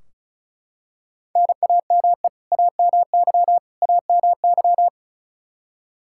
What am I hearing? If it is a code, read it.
Morse code, heard as a keyed tone, about 36 wpm, 700 Hz. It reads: NAME AMY AMY
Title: Practice QSO - Exchange 1 - Beginning - 35wpm